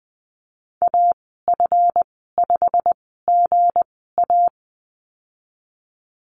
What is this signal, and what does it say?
Morse code, heard as a keyed tone, about 19 wpm, spaced as for 14 wpm, 700 Hz. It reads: AF5GA
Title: 161 Lesson 26g - Letter g - Call Signs 20wpm